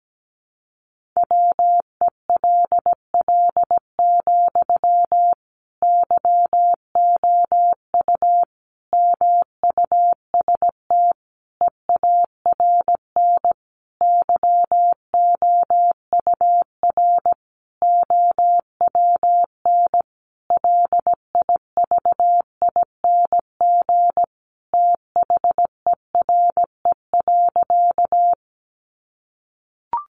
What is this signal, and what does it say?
Morse code, heard as a keyed tone, about 17 wpm, 700 Hz. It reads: WELL, YOU MUST EARN YOUR OWN LIVING THERE.
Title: Sentences from Top 1500 Words 17wpm